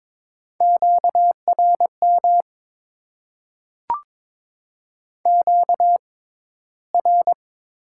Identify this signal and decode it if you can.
Morse code, heard as a keyed tone, about 22 wpm, spaced as for 14 wpm, 700 Hz. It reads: QRM QR
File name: Top 90 QSO Elements - 6x Character Spacing - 22wpm